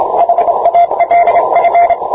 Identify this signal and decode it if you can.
Morse code, heard as a keyed tone, about 26 wpm, 690 Hz, with radio noise. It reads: HALF